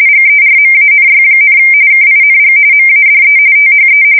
RTTY, about 45.45 baud. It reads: GERMANY, KING MADE THE
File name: fsk170 / usnavy4